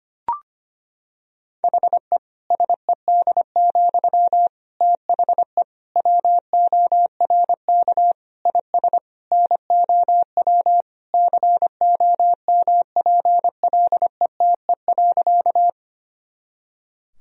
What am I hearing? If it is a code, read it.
Morse code, heard as a keyed tone, about 25 wpm, 700 Hz. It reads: HE SED, THE WORK IS NOW COMPLETE.
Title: Sentences from Top 400 Words and Abbreviations 25wpm